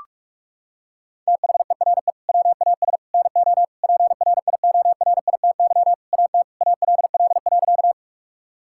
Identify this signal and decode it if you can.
Morse code, heard as a keyed tone, about 46 wpm, 700 Hz. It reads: THERE WAS NO PRIORITY AT ALL.
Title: Sentences from Top 1600 Words 45wpm